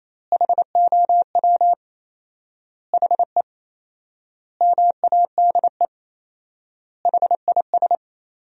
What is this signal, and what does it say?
Morse code, heard as a keyed tone, about 28 wpm, 700 Hz. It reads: HOW HE MADE HIS
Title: 4x Word-Spacing - Sentences from Top 300 Words - 28wpm